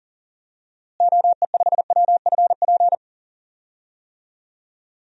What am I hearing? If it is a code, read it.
Morse code, heard as a keyed tone, about 40 wpm, 700 Hz. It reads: OE5WFP
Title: DXCC-100 Call Signs - 6 Characters - Repeated 2x - 40wpm